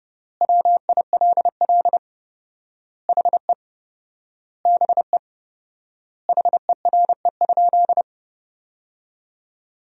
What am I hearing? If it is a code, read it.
Morse code, heard as a keyed tone, about 32 wpm, 700 Hz. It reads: WILL HE BE HERE?
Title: 4x Word-Spacing - Sentences from Top 400 Words - 30wpm